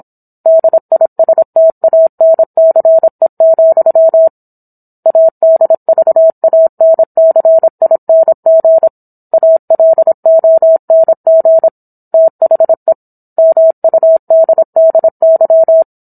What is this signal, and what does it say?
Morse code, heard as a keyed tone, about 27 wpm, 650 Hz. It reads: DISTANCE, ADVANCING ALONG THE MUDDY